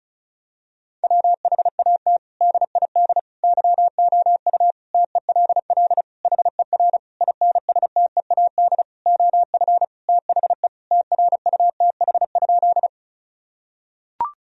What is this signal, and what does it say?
Morse code, heard as a keyed tone, about 34 wpm, 700 Hz. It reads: WHAT DID YOU TELL HER INSTEAD OF THE TRUTH?
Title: Sentences from Top 1000 Words - Review Part 4 - 35wpm